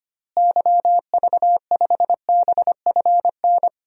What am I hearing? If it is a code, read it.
Morse code, heard as a keyed tone, about 25 wpm, 700 Hz. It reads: YV5BFN
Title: DXCC-100 Call Signs - 6 Characters - Repeated 2x - 25wpm